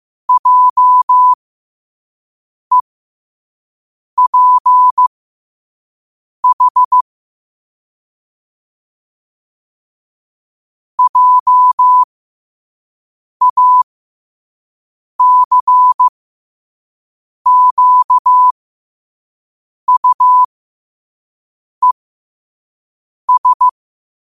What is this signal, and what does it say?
Morse code, heard as a keyed tone, about 15 wpm, spaced as for 5 wpm, 1000 Hz. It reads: JEPH JACQUES
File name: qotdfiles / 5wpm